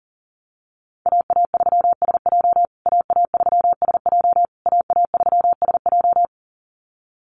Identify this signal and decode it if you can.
Morse code, heard as a keyed tone, about 40 wpm, 700 Hz. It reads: AA3SJ AA3SJ AA3SJ
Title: USA Call Signs Repeated 3x 40wpm